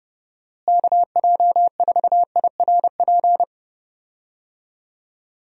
Morse code, heard as a keyed tone, about 32 wpm, 700 Hz. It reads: KJ4IRP
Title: USA Call Signs - 6 Characters - Repeated 3x - 30wpm